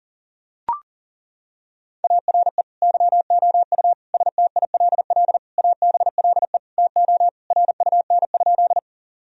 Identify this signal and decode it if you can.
Morse code, heard as a keyed tone, about 40 wpm, 700 Hz. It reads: ARE YOU STILL ABLE TO RUN?
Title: Sentences from Top 1000 Words - Review Part 2 - 40wpm